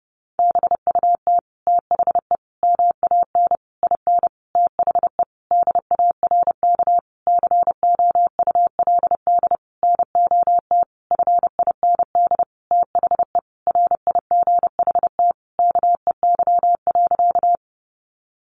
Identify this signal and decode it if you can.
Morse code, heard as a keyed tone, about 30 wpm, 700 Hz. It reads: BUT THE MAN IN THE DARK COULD NOT FIND THE RIGHT KEY.
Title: Sentences from Top 600 Words 30wpm